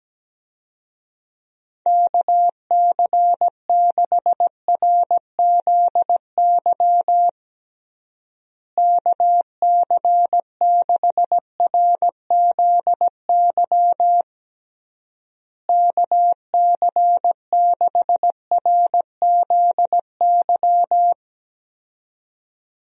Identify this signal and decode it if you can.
Morse code, heard as a keyed tone, about 17 wpm, 700 Hz. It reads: KC6RZY KC6RZY KC6RZY
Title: USA Call Signs - 6 Characters - Repeated 3x - 17wpm